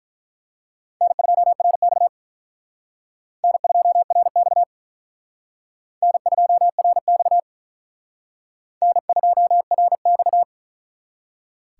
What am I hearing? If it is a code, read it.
Morse code, heard as a keyed tone, about 44 wpm, 700 Hz. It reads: N2RX N2RX N2RX N2RX
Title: Speed Racing - USA 1x2 Call Signs 35wpm